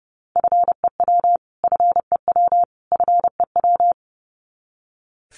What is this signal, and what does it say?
Morse code, heard as a keyed tone, about 28 wpm, 700 Hz. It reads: FEW FEW FEW
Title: Top 300 Words Repeated 30wpm